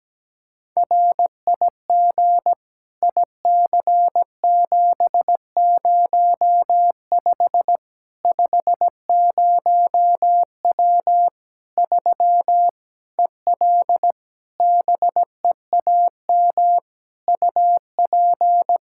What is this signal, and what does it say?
Morse code, heard as a keyed tone, about 17 wpm, 700 Hz. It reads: RIG IC705 50W 3 EL BEAM UP
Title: Practice QSO - Exchange 2 - Info - 17wpm